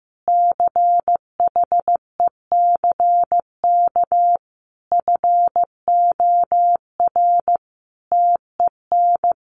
Morse code, heard as a keyed tone, about 15 wpm, 700 Hz. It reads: CHECK FOR TEN
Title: A Sound Of Thunder 15wpm